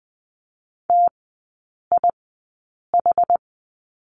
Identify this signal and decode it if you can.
Morse code, heard as a keyed tone, about 20 wpm, spaced as for 8 wpm, 700 Hz. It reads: TIH
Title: 074 Lesson 13e - Letter L - Three Characters 20wpm